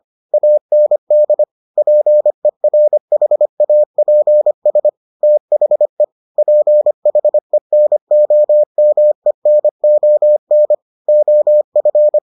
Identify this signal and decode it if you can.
Morse code, heard as a keyed tone, about 25 wpm, 600 Hz. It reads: AND PERHAPS THE PHENOMENON OF